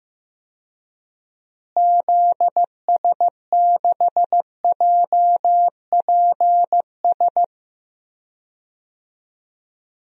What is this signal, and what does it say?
Morse code, heard as a keyed tone, about 15 wpm, 700 Hz. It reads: ZS6JPS
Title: DXCC-100 Call Signs - 6 Characters - Repeated 3x - 15wpm